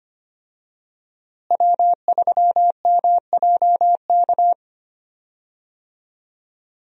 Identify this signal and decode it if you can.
Morse code, heard as a keyed tone, about 25 wpm, 700 Hz. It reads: W3MJK
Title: USA Call Signs - 5 Characters - Repeated 2x - 25wpm